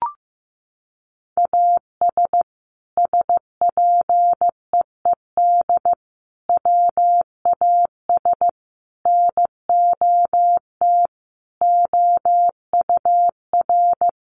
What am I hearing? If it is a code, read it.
Morse code, heard as a keyed tone, about 15 wpm, 700 Hz. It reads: AS SPEED WAS NOT OUR
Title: Sentences from Top 800 Words 15wpm